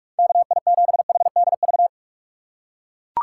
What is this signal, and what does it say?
Morse code, heard as a keyed tone, about 42 wpm, 690 Hz. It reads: KI7HDV
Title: USA Call Signs - 6 Characters - 2x Character Spacing - 45wpm